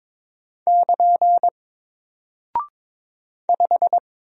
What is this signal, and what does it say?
Morse code, heard as a keyed tone, about 23 wpm, 700 Hz. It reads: <KN> 5
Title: Single Character (Letter:Number:Punctuation:Prosign) Alt 22wpm